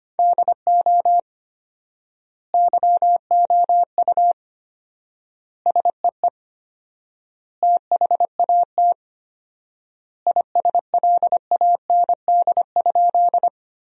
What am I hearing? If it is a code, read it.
Morse code, heard as a keyed tone, about 25 wpm, 700 Hz. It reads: DO YOU SEE THAT ISLAND?
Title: 4x Word-Spacing - Sentences from Top 500 Words - 25wpm